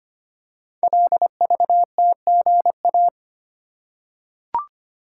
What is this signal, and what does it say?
Morse code, heard as a keyed tone, about 25 wpm, 700 Hz. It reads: LVTGA
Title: Code Groups - 5 Letters - 25wpm